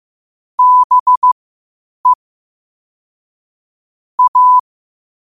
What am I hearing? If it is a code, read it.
Morse code, heard as a keyed tone, about 15 wpm, spaced as for 8 wpm, 1000 Hz. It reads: BE A
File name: qotdfiles / 8wpm